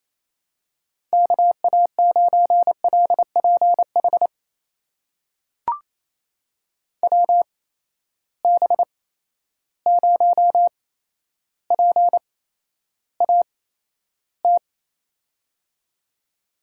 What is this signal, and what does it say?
Morse code, heard as a keyed tone, about 28 wpm, spaced as for 12 wpm, 700 Hz. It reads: KA9LPH WB0PAT
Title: USA Call Signs - 6 Characters - 8x Character Spacing - 28wpm